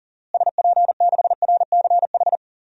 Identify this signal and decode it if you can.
Morse code, heard as a keyed tone, about 40 wpm, 700 Hz. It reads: SP6RCH
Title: DXCC-100 Call Signs - 6 Characters - Repeated 3x - 40wpm